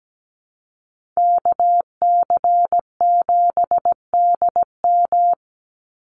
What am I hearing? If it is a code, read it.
Morse code, heard as a keyed tone, about 17 wpm, 700 Hz. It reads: KC7DM
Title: NA SOTA Activator Callsigns 17wpm